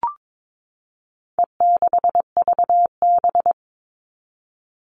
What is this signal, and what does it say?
Morse code, heard as a keyed tone, about 22 wpm, 700 Hz. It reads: E6VB
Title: All Cut Numbers - 4-Digit 22wpm